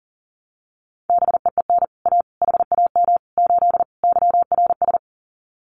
Morse code, heard as a keyed tone, about 40 wpm, 700 Hz. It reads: BEEN A HAM 8 YRS
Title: Common QSO Phrases Encore 40wpm